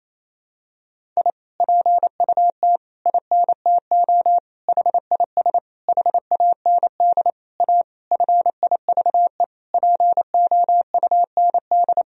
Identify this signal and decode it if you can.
Morse code, heard as a keyed tone, about 28 wpm, 700 Hz. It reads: I PUT INTO HIS HAND A FIVE POUND
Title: Sentences from Top 1000 Words - Review Part 2 - 28wpm